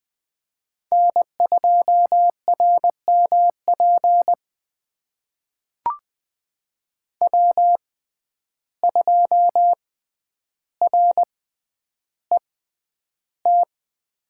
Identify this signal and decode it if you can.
Morse code, heard as a keyed tone, about 20 wpm, spaced as for 10 wpm, 700 Hz. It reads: N2RMP W2RET
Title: USA Call Signs - 5 Characters - 6x Character Spacing - 20wpm